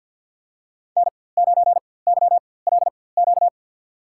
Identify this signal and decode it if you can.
Morse code, heard as a keyed tone, about 53 wpm, spaced as for 24 wpm, 700 Hz. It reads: N9YLQ
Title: USA Call Signs - 5 Characters - 4x Character Spacing - 50wpm